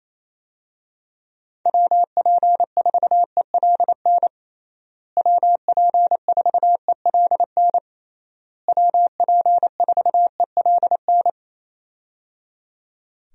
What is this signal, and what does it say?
Morse code, heard as a keyed tone, about 28 wpm, 700 Hz. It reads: WP4ELN WP4ELN WP4ELN
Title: USA Call Signs - 6 Characters - Repeated 3x - 28wpm